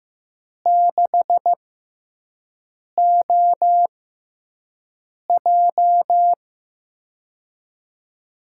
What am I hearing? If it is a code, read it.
Morse code, heard as a keyed tone, about 16 wpm, spaced as for 5 wpm, 700 Hz. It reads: 6OJ
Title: USA Call Signs - 5 Characters - 6x Character Spacing - 15wpm